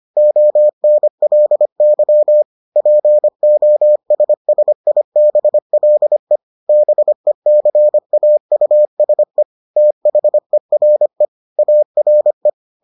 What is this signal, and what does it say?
Morse code, heard as a keyed tone, about 25 wpm, 600 Hz. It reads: ONLY POSSIBLE BECAUSE THERE ARE